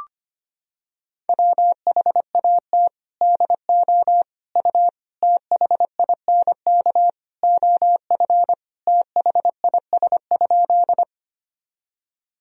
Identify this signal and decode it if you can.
Morse code, heard as a keyed tone, about 25 wpm, 700 Hz. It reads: WHAT DO U THINK OF THIS?
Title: Sentences from Top 200 Words and Abbreviations 25wpm